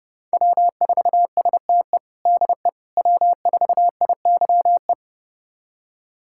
Morse code, heard as a keyed tone, about 30 wpm, 700 Hz. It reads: W4STE DE W4IYE
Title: USA Callsign DE Callsign - 5 Characters - 30wpm